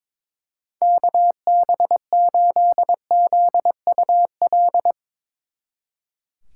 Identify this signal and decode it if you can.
Morse code, heard as a keyed tone, about 22 wpm, 700 Hz. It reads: KB8ZUL
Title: Speed Racing - USA 2x3 Call Signs 22wpm